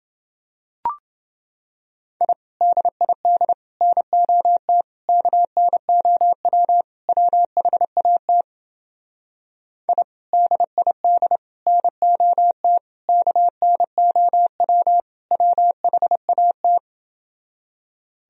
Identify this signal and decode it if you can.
Morse code, heard as a keyed tone, about 29 wpm, 700 Hz. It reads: I DID NOT KNOW WHAT I DID NOT KNOW WHAT